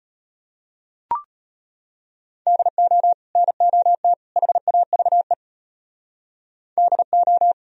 Morse code, heard as a keyed tone, about 30 wpm, 700 Hz. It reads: DO NOT HAVE DO
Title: Speed-Racing - Sets of 3 Words 25wpm